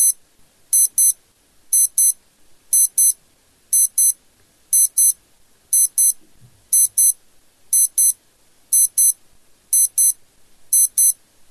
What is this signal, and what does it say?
There is no code.